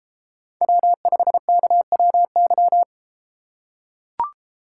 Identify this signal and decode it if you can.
Morse code, heard as a keyed tone, about 31 wpm, 700 Hz. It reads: W5KWY